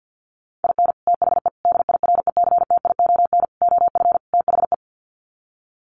Morse code, heard as a keyed tone, about 50 wpm, 700 Hz. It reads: IN THE DIRECTION OF THE
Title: Sets of 5 Words 50wpm